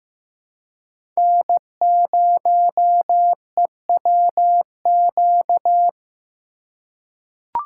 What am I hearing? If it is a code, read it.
Morse code, heard as a keyed tone, about 15 wpm, 700 Hz. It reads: N0EWQ
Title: USA Call Signs - 5 Characters - 4x Character Spacing - 15wpm